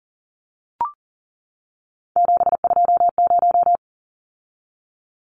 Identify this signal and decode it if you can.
Morse code, heard as a keyed tone, about 39 wpm, 700 Hz. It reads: 720
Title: Numbers - 3-Digit 40wpm